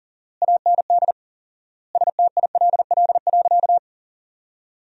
Morse code, heard as a keyed tone, about 40 wpm, 700 Hz. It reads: AND STILL.
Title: 4x Word-Spacing - Sentences from Top 500 Words - 40wpm